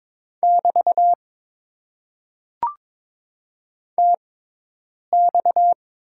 Morse code, heard as a keyed tone, about 22 wpm, spaced as for 8 wpm, 700 Hz. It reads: <BT> TX
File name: Top 90 QSO Elements - 6x Character Spacing - 22wpm